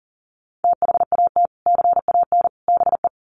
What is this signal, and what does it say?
Morse code, heard as a keyed tone, about 40 wpm, 700 Hz. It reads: THAT CAN BE
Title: Sets of 3 Words 40wpm